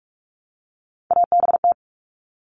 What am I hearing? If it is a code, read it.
Morse code, heard as a keyed tone, about 44 wpm, 700 Hz. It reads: ABT
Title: Abbreviations 45wpm